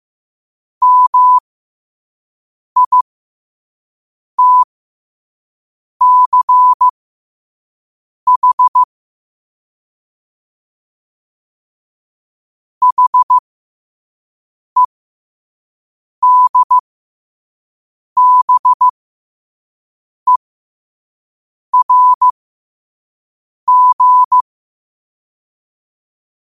Morse code, heard as a keyed tone, about 15 wpm, spaced as for 5 wpm, 1000 Hz. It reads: MITCH HEDBERG
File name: qotdfiles / 5wpm